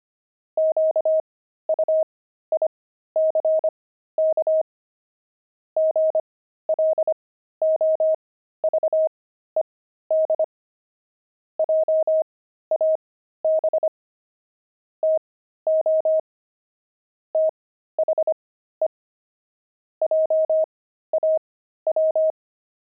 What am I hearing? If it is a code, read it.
Morse code, heard as a keyed tone, about 25 wpm, spaced as for 13 wpm, 630 Hz. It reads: QUICK GLOVED JAB TO THE JAW